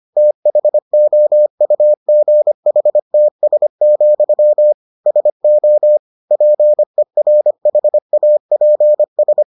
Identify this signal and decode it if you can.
Morse code, heard as a keyed tone, about 25 wpm, 600 Hz. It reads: THOUGHTS, SO PERHAPS